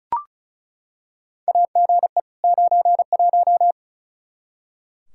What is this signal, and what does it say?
Morse code, heard as a keyed tone, about 36 wpm, 700 Hz. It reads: AGE 91
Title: Practice QSO - Exchange 3 - More Info - 35wpm